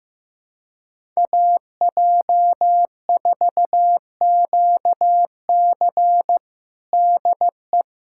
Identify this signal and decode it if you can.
Morse code, heard as a keyed tone, about 15 wpm, 700 Hz. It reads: AJ4QC DE
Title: USA Callsign DE Callsign - 5 Characters - 15wpm